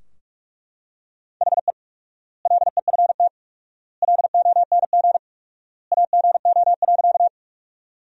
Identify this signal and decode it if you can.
Morse code, heard as a keyed tone, about 43 wpm, 690 Hz. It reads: HE LEFT LONG AGO.
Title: 4x Word-Spacing - Sentences from Top 500 Words - 45wpm